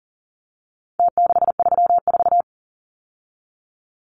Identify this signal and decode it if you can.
Morse code, heard as a keyed tone, about 40 wpm, 700 Hz. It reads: T634
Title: Numbers - Contest Style - 4-Digit 40wpm